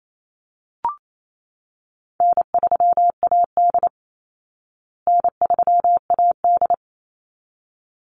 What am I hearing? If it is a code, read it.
Morse code, heard as a keyed tone, about 28 wpm, 690 Hz. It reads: N3AD N3AD
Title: Call Signs Repeated 3x - 28wpm